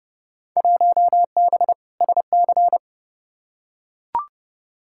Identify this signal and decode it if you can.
Morse code, heard as a keyed tone, about 31 wpm, 700 Hz. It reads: 1B SC
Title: Field Day Exchanges Encore - 30wpm